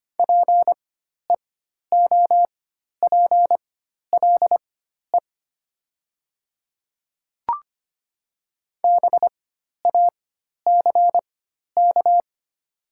Morse code, heard as a keyed tone, about 25 wpm, spaced as for 12 wpm, 700 Hz. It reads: PEOPLE BACK
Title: 4x Character Spacing - Top 200 Words - 25wpm